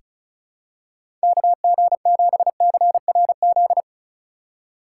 Morse code, heard as a keyed tone, about 36 wpm, 700 Hz. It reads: KG7CRZ
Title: USA Call Signs - 8x Character Spacing - 35wpm